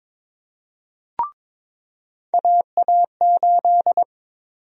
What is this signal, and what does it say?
Morse code, heard as a keyed tone, about 21 wpm, 700 Hz. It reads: AA8
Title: USA Call Sign Prefixes - 22wpm